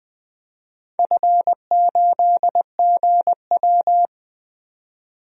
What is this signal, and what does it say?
Morse code, heard as a keyed tone, about 20 wpm, 700 Hz. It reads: F8GW
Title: DXCC-100 Call Signs - 4 Characters - 20wpm